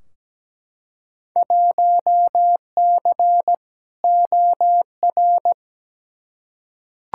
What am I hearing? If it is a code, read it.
Morse code, heard as a keyed tone, about 17 wpm, 700 Hz. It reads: 1C OR
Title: Field Day Exchanges - 17wpm